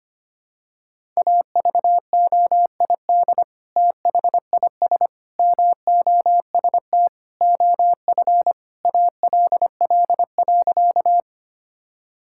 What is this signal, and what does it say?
Morse code, heard as a keyed tone, about 25 wpm, 700 Hz. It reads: AVOID THIS MOST OF ALL.
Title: Sentences from Top 1000 Words - Review Part 6 - 25wpm